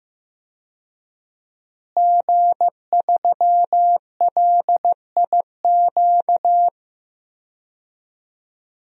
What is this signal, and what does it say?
Morse code, heard as a keyed tone, about 15 wpm, 700 Hz. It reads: G3LIQ
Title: DXCC-100 Call Signs - 5 Characters - Repeated 3x - 15wpm